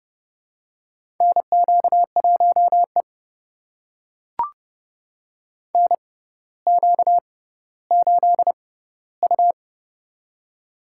Morse code, heard as a keyed tone, about 30 wpm, spaced as for 15 wpm, 700 Hz. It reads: NQ1E NQ8U
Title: USA Call Signs - 4 Characters - 6x Character Spacing - 30wpm